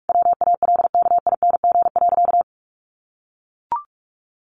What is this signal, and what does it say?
Morse code, heard as a keyed tone, about 44 wpm, 700 Hz. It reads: WALKING.
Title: Easy Phrases 45wpm